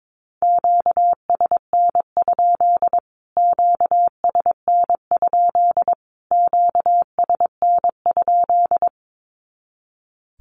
Morse code, heard as a keyed tone, about 22 wpm, 700 Hz. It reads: QSN? QSN? QSN?
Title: Q-Codes Repeated 3x 22wpm